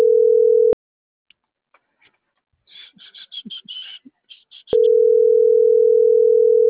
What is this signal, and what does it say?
Ringback tone (North America), 440 + 480 Hz.